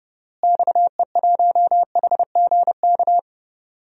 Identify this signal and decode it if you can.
Morse code, heard as a keyed tone, about 30 wpm, 700 Hz. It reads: XE1HGK
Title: DXCC-100 Call Signs - 6 Characters - Repeated 3x - 30wpm